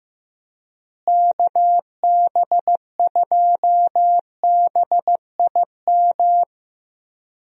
Morse code, heard as a keyed tone, about 15 wpm, 700 Hz. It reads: KB2BIM
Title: USA Call Signs - 6 Characters - Repeated 2x - 15wpm